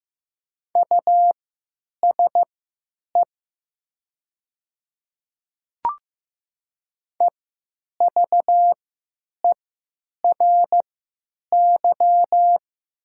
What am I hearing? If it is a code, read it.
Morse code, heard as a keyed tone, about 15 wpm, spaced as for 8 wpm, 700 Hz. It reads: USE EVERY
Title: 3x Character Spacing - Top 300 Words - 15wpm